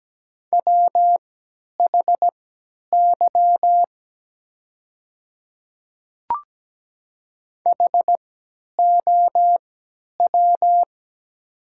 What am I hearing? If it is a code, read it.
Morse code, heard as a keyed tone, about 17 wpm, spaced as for 10 wpm, 690 Hz. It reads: WHY HOW
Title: 3x Character Spacing - Top 300 Words - 17wpm